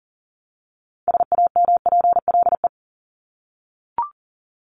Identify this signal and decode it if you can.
Morse code, heard as a keyed tone, about 40 wpm, 700 Hz. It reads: SAMPLE
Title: Top 1501-1750 Words 40wpm